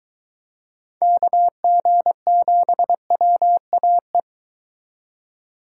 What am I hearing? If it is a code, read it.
Morse code, heard as a keyed tone, about 22 wpm, 700 Hz. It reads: KG7WAE